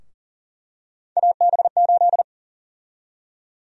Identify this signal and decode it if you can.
Morse code, heard as a keyed tone, about 40 wpm, 700 Hz. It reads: AB8
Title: USA Call Sign Prefixes - 40wpm